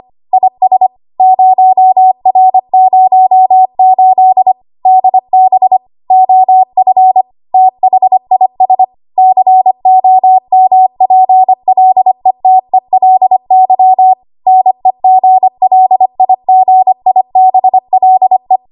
Morse code, heard as a keyed tone, about 25 wpm, 750 Hz. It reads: IS 0R08 DB OF THIS COMPLETELY NEGLIGIBLE
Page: W1AW Web Code Practice Files